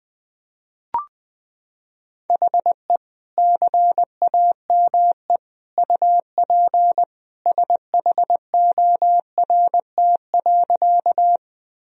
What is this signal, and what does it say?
Morse code, heard as a keyed tone, about 20 wpm, 700 Hz. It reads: HE CAME UP SHORT.
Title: Sentences from Top 1000 Words - Review Part 2 - 20wpm